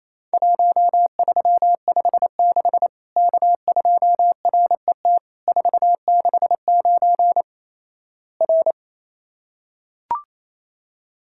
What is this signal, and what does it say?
Morse code, heard as a keyed tone, about 27 wpm, 700 Hz. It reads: 1356 K2RET 469 R
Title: Worked All Europe DX Contest - QTC Exchange - 28wpm